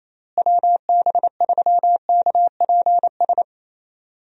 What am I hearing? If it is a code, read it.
Morse code, heard as a keyed tone, about 28 wpm, 700 Hz. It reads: WB3KPS
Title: USA Call Signs - 6 Characters - Repeated 2x - 28wpm